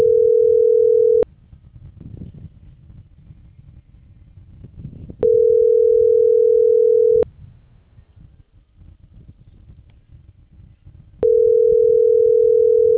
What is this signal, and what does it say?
Ringback tone (North America), 440 + 480 Hz.